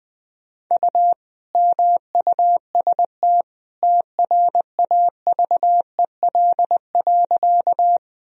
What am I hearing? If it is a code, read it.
Morse code, heard as a keyed tone, about 20 wpm, 700 Hz. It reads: U MUST TRAVEL.
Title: Sentences from Top 400 Words and Abbreviations 20wpm